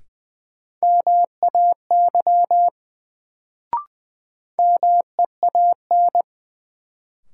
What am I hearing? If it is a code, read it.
Morse code, heard as a keyed tone, about 21 wpm, 700 Hz. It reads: MAY MEAN
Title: Sets of 2 Words - New Word Drill - 20wpm